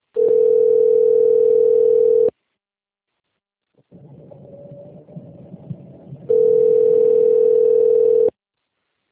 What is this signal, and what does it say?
Ringback tone (North America), 440 + 480 Hz.